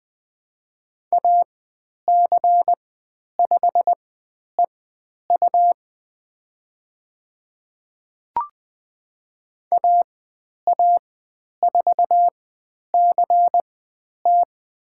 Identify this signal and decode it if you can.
Morse code, heard as a keyed tone, about 21 wpm, spaced as for 10 wpm, 700 Hz. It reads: AC5EU AA4CT
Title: 102 Lesson 17g - Letter u - Call Signs 20wpm